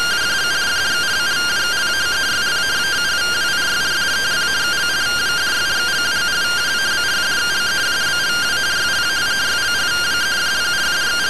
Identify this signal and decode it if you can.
Two-tone teleprinter signal, about 170 Hz shift.